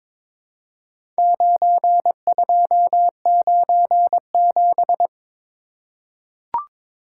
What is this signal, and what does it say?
Morse code, heard as a keyed tone, about 22 wpm, 700 Hz. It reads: 9297
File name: practice / 22wpm / Four Letter-Number Combinations - 22wpm